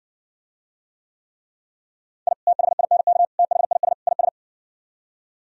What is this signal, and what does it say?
Morse code, heard as a keyed tone, about 59 wpm, 700 Hz. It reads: I THINK THIS IS